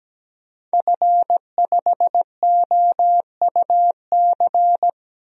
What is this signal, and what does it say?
Morse code, heard as a keyed tone, about 17 wpm, 700 Hz. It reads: F5OUC
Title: DXCC-100 Call Signs - 5 Characters - Repeated 2x - 17wpm